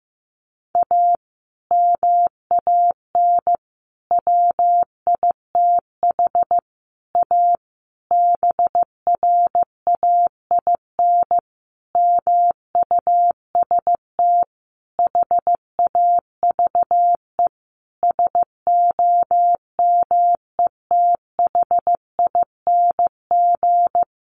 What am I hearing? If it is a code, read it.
Morse code, heard as a keyed tone, about 15 wpm, 700 Hz. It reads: A MAN WITH A BRAIN MUST HAVE SOMETHING
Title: Sentences from Top 1100 Words 15wpm